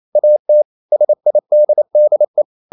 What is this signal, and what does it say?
Morse code, heard as a keyed tone, about 28 wpm, 600 Hz. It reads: AT SIDDE
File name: QRQ 28wpm